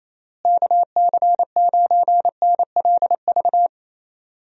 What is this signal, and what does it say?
Morse code, heard as a keyed tone, about 28 wpm, 700 Hz. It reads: KC9NLV
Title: USA Call Signs - 6 Characters - Repeated 3x - 28wpm